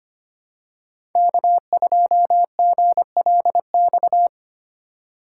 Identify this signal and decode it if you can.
Morse code, heard as a keyed tone, about 25 wpm, 700 Hz. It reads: K2GLX
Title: USA Call Signs - 5 Characters - 2x Character Spacing - 25wpm